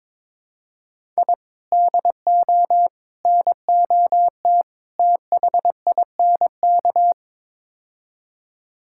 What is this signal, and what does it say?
Morse code, heard as a keyed tone, about 22 wpm, 700 Hz. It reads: I DO NOT THINK
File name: practice / 22wpm / Build A Phrase - 5-Word Phrases - 22wpm